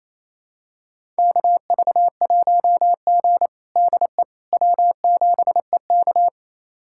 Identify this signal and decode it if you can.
Morse code, heard as a keyed tone, about 28 wpm, 700 Hz. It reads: KV1G DE W7EK
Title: USA Callsign DE Callsign - 4 Characters - 28wpm